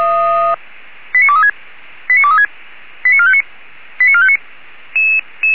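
Five-tone selective call (ZVEI), heard as quick runs of five tones, tones of about 70 ms. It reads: 89236 89236 89459 89459
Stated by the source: ZVEI five-tone selective call 89401, received by SDR.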